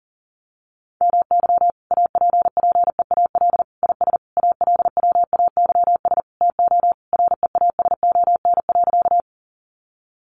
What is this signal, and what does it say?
Morse code, heard as a keyed tone, about 40 wpm, 700 Hz. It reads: MY APPEAL IS ALWAYS TO REASON.
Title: Sentences from Top 1300 Words 40wpm